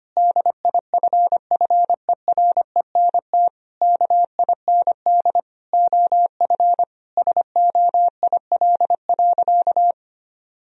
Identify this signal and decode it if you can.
Morse code, heard as a keyed tone, about 25 wpm, 700 Hz. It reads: DIFFERENT KIND OF SOIL.
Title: Sentences from Top 1000 Words - Review Part 4 - 25wpm